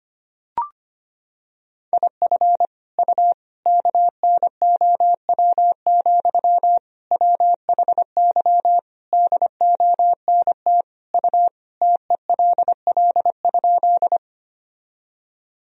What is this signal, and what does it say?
Morse code, heard as a keyed tone, about 25 wpm, 700 Hz. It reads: IF U KNOW, WHY DONT U TELL?
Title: Sentences from Top 300 Words and Abbreviations 25wpm